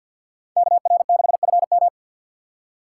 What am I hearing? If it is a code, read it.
Morse code, heard as a keyed tone, about 51 wpm, 700 Hz. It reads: KR6FM
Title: USA Call Signs - 5 Characters - 2x Character Spacing - 50wpm